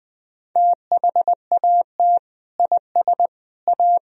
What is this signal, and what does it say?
Morse code, heard as a keyed tone, about 21 wpm, 700 Hz. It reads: THAT IS A
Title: Sentences from Top 1000 Words - Review Part 3 - 20wpm